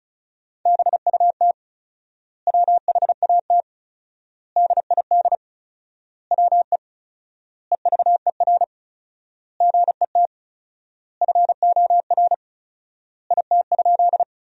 Morse code, heard as a keyed tone, about 36 wpm, 700 Hz. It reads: BUT WHAT DID WE EVER GET FOR IT?
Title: 4x Word-Spacing - Sentences from Top 400 Words - 35wpm